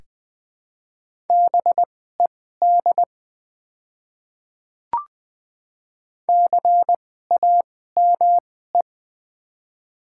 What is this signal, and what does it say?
Morse code, heard as a keyed tone, about 21 wpm, spaced as for 15 wpm, 700 Hz. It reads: BED CAME
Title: 2x Character Spacing - Top 500 Words - 20wpm